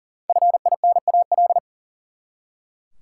Morse code, heard as a keyed tone, about 37 wpm, 700 Hz. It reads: FINAL
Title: Five Letter Words - 40wpm